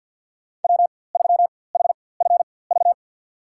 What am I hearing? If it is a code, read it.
Morse code, heard as a keyed tone, about 50 wpm, spaced as for 23 wpm, 700 Hz. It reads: W3HFV
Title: USA Call Signs - 5 Characters - 4x Character Spacing - 50wpm